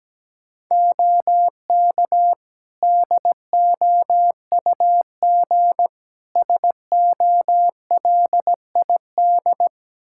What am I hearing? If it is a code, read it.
Morse code, heard as a keyed tone, about 17 wpm, 700 Hz. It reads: OK DOUG SOLID
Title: K1USN SST QSO Practice 17wpm